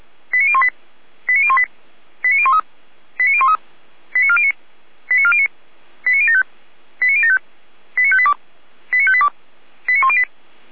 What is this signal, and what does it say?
Five-tone selective call (ZVEI), heard as quick runs of five tones, tones of about 70 ms. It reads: 89018 89018 89013 89013 89409 89409 89075 89075 89582 89582 89108